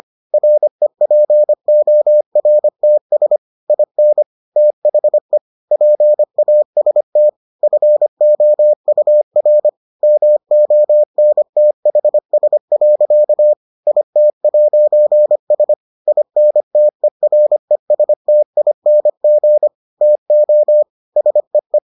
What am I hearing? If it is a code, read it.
Morse code, heard as a keyed tone, about 25 wpm, 600 Hz. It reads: REPORTS IN THE PAST FOUR MONTHS. IT'S INTERESTING TO SEE